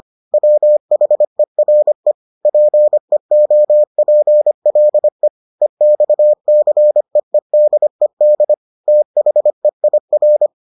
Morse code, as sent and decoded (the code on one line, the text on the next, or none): .-- .... .-. / .--. --- .--. .-.. / . -..- -.-. . -.. -.. / - .... .. .-.
WHERE PEOPLE EXCEEDED THEIR